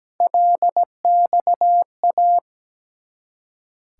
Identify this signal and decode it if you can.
Morse code, heard as a keyed tone, about 17 wpm, 690 Hz. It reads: LXA
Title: Ontario QSO Party Exchange 17wpm